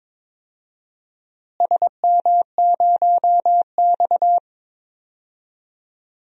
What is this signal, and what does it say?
Morse code, heard as a keyed tone, about 21 wpm, 700 Hz. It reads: SM0X
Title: DXCC-100 Call Signs - 4 Characters - Repeated 3x - 22wpm